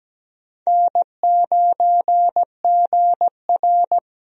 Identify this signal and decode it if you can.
Morse code, heard as a keyed tone, about 17 wpm, 700 Hz. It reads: N9GR
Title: USA Call Signs - 4 Characters - Repeated 3x - 17wpm